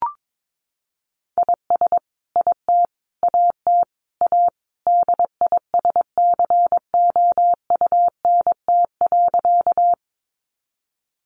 Morse code, heard as a keyed tone, about 22 wpm, 700 Hz. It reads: IS IT AT A DISCOUNT.
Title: Sentences from Top 1200 Words 22wpm